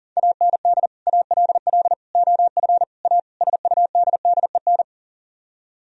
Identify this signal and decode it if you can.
Morse code, heard as a keyed tone, about 51 wpm, 700 Hz. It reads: AND ALL OF A SUDDEN